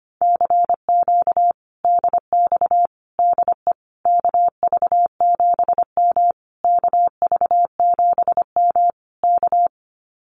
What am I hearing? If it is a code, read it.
Morse code, heard as a keyed tone, about 25 wpm, 700 Hz. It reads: CQ DX DE KV7M KV7M K
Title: Common QSO Phrases Encore 25wpm